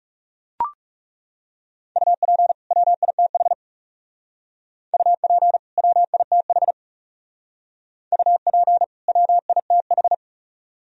Morse code, heard as a keyed tone, about 39 wpm, 700 Hz. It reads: UP WITH UP WITH UP WITH